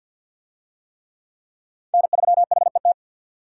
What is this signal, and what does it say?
Morse code, heard as a keyed tone, about 51 wpm, 700 Hz. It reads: N3HET
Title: USA Call Signs - 5 Characters - 50wpm